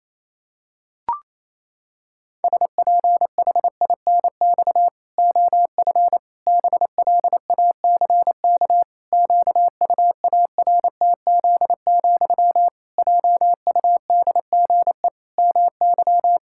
Morse code, heard as a keyed tone, about 28 wpm, 700 Hz. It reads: SPHINX OF BLACK QUARTZ, JUDGE MY
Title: Pangram Challenge v2 - 28wpm